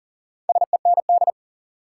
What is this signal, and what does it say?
Morse code, heard as a keyed tone, about 40 wpm, 700 Hz. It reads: SEND
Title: Sentences from Top 1500 Words - New Word Drill Alt - 40wpm